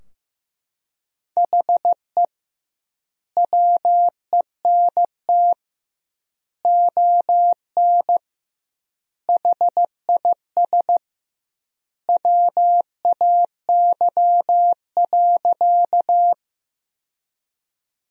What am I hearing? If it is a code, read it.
Morse code, heard as a keyed tone, about 15 wpm, 700 Hz. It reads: HE WENT ON HIS WAY.
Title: 2x Word-Spacing - Sentences from Top 200 Words - 15wpm